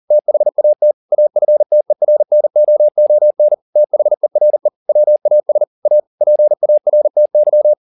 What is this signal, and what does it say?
Morse code, heard as a keyed tone, about 40 wpm, 600 Hz. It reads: THAT AFTERNOON THERE WAS A PARTY
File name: the old man and the sea 2 40wpm